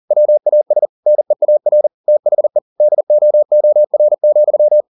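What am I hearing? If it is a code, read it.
Morse code, heard as a keyed tone, about 40 wpm, 600 Hz. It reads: WAS NEAR THE DOOR,
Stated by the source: Morse code at 40 wpm, practice text, CW med Gnister.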